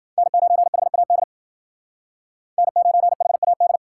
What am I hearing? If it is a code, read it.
Morse code, heard as a keyed tone, about 53 wpm, 700 Hz. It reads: N9HAD N9HAD